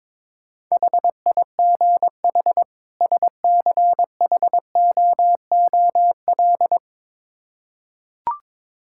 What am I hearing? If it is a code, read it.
Morse code, heard as a keyed tone, about 21 wpm, 700 Hz. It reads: HIGH SCHOOL
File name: Build A Phrase - 2-Word Phrases - 22wpm